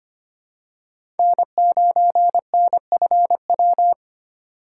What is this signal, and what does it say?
Morse code, heard as a keyed tone, about 25 wpm, 700 Hz. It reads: N9NFW
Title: USA Call Signs - 4x Character Spacing - 25wpm